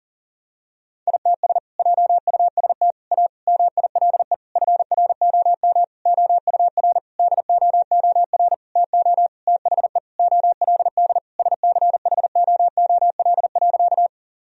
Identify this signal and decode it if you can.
Morse code, heard as a keyed tone, about 37 wpm, 700 Hz. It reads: ITS JUST A MILE FROM OUR DOOR TO THE OLD SCHOOL.
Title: Sentences from Top 1000 Words - Review Part 1 - 40wpm